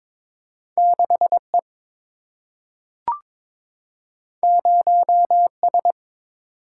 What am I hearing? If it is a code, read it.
Morse code, heard as a keyed tone, about 22 wpm, 700 Hz. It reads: 6E 0S
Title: Single-Letter Single-Number 22wpm